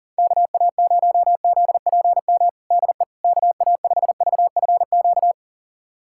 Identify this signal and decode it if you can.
Morse code, heard as a keyed tone, about 37 wpm, 700 Hz. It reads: KA0ZPM DE KA5VFQ
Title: USA Callsign DE Callsign - 6 Characters - 40wpm